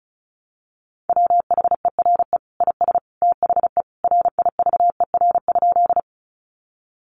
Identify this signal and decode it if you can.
Morse code, heard as a keyed tone, about 34 wpm, 700 Hz. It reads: WHERE IS THE RIVER?
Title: Sentences from Top 300 Words 35wpm